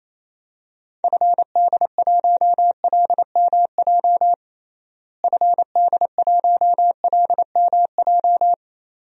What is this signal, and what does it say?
Morse code, heard as a keyed tone, about 28 wpm, 700 Hz. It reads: FD1LMJ FD1LMJ
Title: DXCC-100 Call Signs - 6 Characters - Repeated 2x - 28wpm